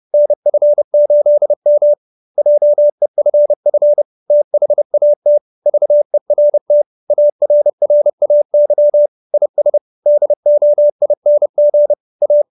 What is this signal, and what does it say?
Morse code, heard as a keyed tone, about 30 wpm, 590 Hz. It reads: NF8M JEFF THAT VERT ARRAY IS DOING A